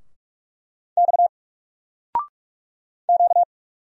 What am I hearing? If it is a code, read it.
Morse code, heard as a keyed tone, about 44 wpm, 700 Hz. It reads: X Q
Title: Single Letter-Number Alt 45wpm